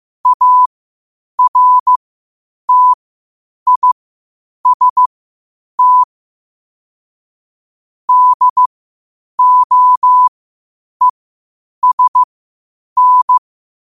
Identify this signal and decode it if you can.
Morse code, heard as a keyed tone, about 15 wpm, spaced as for 8 wpm, 1000 Hz. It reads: ARTIST DOESN
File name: qotdfiles / 8wpm